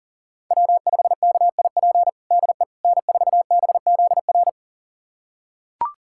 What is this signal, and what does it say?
Morse code, heard as a keyed tone, about 40 wpm, 700 Hz. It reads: W5KIP DE N4BZR
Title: USA Callsign DE Callsign - 5 Characters - 40wpm